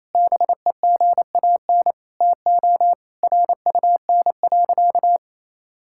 Morse code, heard as a keyed tone, about 28 wpm, 700 Hz. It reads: BEGAN TO RUN.
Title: Sentences from Top 1600 Words 28wpm